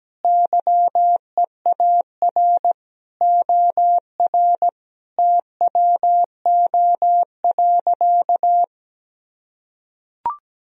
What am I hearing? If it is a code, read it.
Morse code, heard as a keyed tone, about 17 wpm, 700 Hz. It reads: YEAR OR TWO.
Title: Sentences from Top 1000 Words - Review Part 1 - 17wpm